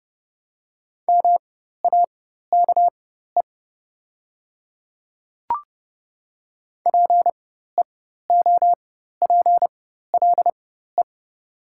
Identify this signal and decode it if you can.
Morse code, heard as a keyed tone, about 30 wpm, spaced as for 14 wpm, 700 Hz. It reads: MAKE PEOPLE
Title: 4x Character Spacing - Top 200 Words - 30wpm